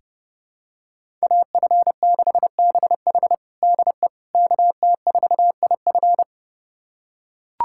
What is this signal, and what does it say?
Morse code, heard as a keyed tone, about 32 wpm, 700 Hz. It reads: AF6BH DE KT4IF
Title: USA Callsign DE Callsign - 5 Characters - 30wpm